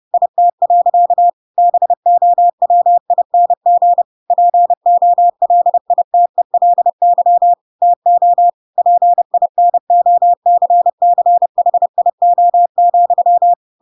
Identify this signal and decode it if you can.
Morse code, heard as a keyed tone, about 30 wpm, 700 Hz. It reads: IT. BOWING POLITELY TO PINOCCHIO,
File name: Pinocchio / Pinocchio ch5 30 30 ik1hge